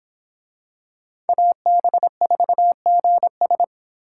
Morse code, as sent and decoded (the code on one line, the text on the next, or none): .- -... ....- --. ...
AB4GS